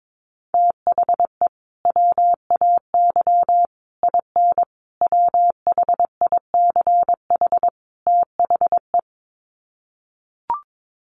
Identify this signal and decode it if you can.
Morse code, heard as a keyed tone, about 22 wpm, 700 Hz. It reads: THE WAY IN WHICH THE
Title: Sets of 5 Words Encore2 22wpm